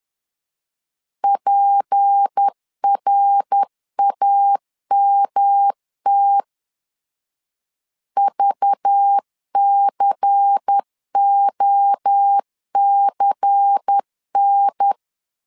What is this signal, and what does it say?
Morse code, heard as a keyed tone, about 10 wpm, 800 Hz. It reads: PRAMT VCOCN
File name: LEZIONI / 6 PKX0 / 6 P 10wpm 2432